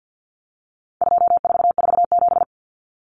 Morse code, heard as a keyed tone, about 50 wpm, 700 Hz. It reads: 2447
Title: Numbers - 4-Digit 50wpm